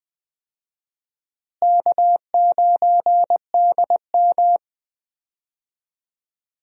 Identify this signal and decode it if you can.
Morse code, heard as a keyed tone, about 21 wpm, 700 Hz. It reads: K9DM
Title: USA Call Signs - 4 Characters - Repeated 3x - 20wpm